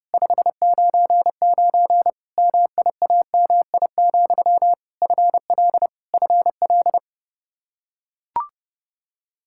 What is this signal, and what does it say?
Morse code, heard as a keyed tone, about 30 wpm, 700 Hz. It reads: 599 MIAMI, FL FL
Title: Practice QSO - Exchange 1 - Beginning - 30wpm